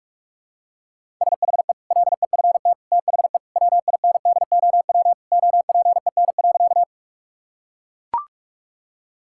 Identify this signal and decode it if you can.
Morse code, heard as a keyed tone, about 43 wpm, 700 Hz. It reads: SHE LEFT THE WINDOW OPEN.
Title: Sentences from Top 1600 Words 45wpm